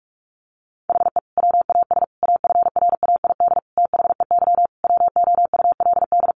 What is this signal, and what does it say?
Morse code, heard as a keyed tone, about 44 wpm, 700 Hz. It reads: HE WAS AFRAID THEY WOULD
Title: Sentences from Top 1500 Words 45wpm